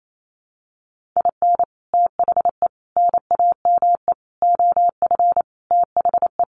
Morse code, heard as a keyed tone, about 28 wpm, 700 Hz. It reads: IN THE NAME OF THE
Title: Sentences from Top 1500 Words 28wpm